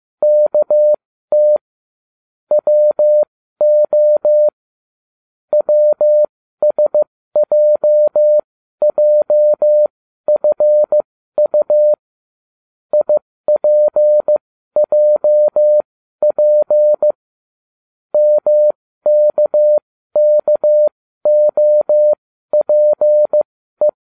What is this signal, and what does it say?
Morse code, heard as a keyed tone, about 15 wpm, spaced as for 12 wpm, 600 Hz. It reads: KT WO WSJJFU IPJP MKKOPE